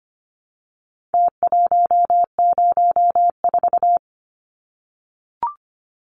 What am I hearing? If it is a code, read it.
Morse code, heard as a keyed tone, about 25 wpm, 700 Hz. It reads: T104
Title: Numbers - Contest Style - 4-Digit 25wpm